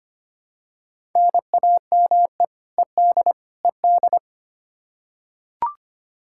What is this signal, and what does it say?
Morse code, heard as a keyed tone, about 25 wpm, 700 Hz. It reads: NAME ED ED
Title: Practice QSO - Exchange 1 - Beginning - 25wpm